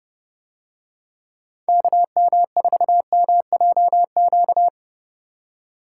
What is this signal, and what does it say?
Morse code, heard as a keyed tone, about 28 wpm, 700 Hz. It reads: KM4MJQ